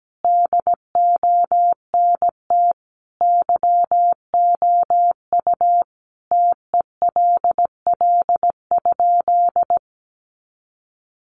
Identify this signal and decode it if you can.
Morse code, heard as a keyed tone, about 17 wpm, 700 Hz. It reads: DONT YOU TELL?
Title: Sentences from Top 300 Words 17wpm